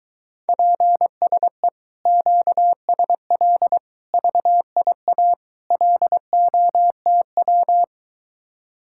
Morse code, heard as a keyed tone, about 22 wpm, 700 Hz. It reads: PSE QSL VIA LOTW